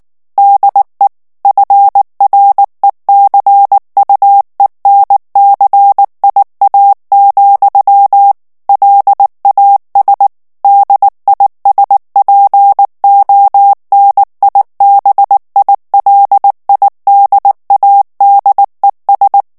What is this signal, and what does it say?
Morse code, heard as a keyed tone, about 19 wpm, 800 Hz. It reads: DE FRECUENCIA, LAS DISPONIBILIDADES
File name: morse mp3 / ESPAGNOL TEXTE EN CLAIR 2 VITESSE 18 MOTS MINUTES